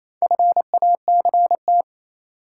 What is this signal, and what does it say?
Morse code, heard as a keyed tone, about 28 wpm, 700 Hz. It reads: FACT
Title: Sets of 2 Words - New Word Drill - 28wpm